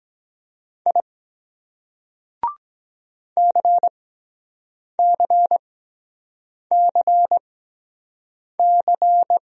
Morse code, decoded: I CCCC

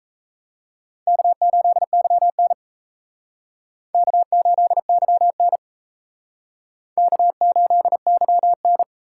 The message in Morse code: K8YN K8YN K8YN

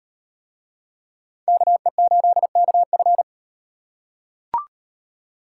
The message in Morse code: KE8KF

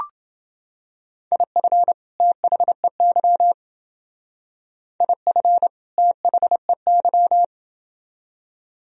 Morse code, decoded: IF THEY IF THEY

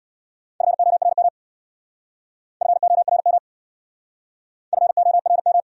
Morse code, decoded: FORM FORM FORM

